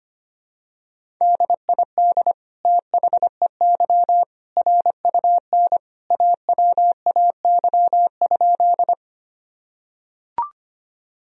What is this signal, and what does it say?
Morse code, heard as a keyed tone, about 25 wpm, 700 Hz. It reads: DID THEY RUN AWAY?
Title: Sentences from Top 1000 Words - Review Part 1 - 25wpm